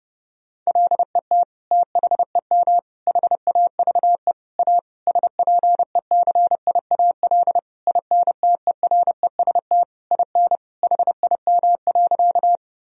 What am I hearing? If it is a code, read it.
Morse code, heard as a keyed tone, about 30 wpm, 700 Hz. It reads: LET THEM HAVE A SPECIAL INTEREST IN HIM.
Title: Sentences from Top 1000 Words - Review Part 3 - 30wpm